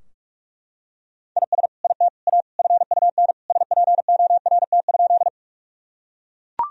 Morse code, heard as a keyed tone, about 42 wpm, 700 Hz. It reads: IS IT A FUN SPORT?
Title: Sentences from Top 1000 Words - Review Part 4 - 45wpm